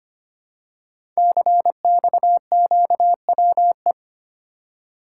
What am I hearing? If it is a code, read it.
Morse code, heard as a keyed tone, about 25 wpm, 700 Hz. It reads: CXQWE
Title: Code Groups - 5 Letters - 25wpm